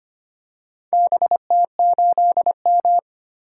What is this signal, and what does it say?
Morse code, heard as a keyed tone, about 25 wpm, 700 Hz. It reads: BT8M
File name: practice / 25wpm / Four Letter-Number Combinations - 25wpm